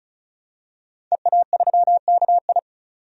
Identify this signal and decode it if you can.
Morse code, heard as a keyed tone, about 36 wpm, 700 Hz. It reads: EA3KI
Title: DXCC-100 Call Signs - 5 Characters - 35wpm